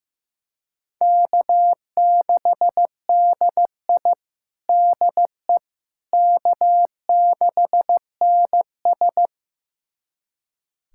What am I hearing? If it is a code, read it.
Morse code, heard as a keyed tone, about 15 wpm, 700 Hz. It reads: K6DI DE K6NS
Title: USA Callsign DE Callsign - 4 Characters - 15wpm